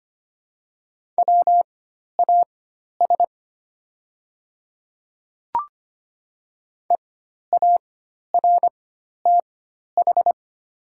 Morse code, heard as a keyed tone, about 25 wpm, spaced as for 12 wpm, 700 Hz. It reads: WAS EARTH
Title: 4x Character Spacing - Top 300 Words - 25wpm